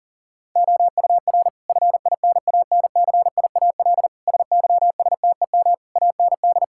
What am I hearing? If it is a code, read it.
Morse code, heard as a keyed tone, about 40 wpm, 690 Hz. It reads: OUR FINANCIAL SYSTEM AND
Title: Sentences from Top 1000 Words - Review Part 5 - 40wpm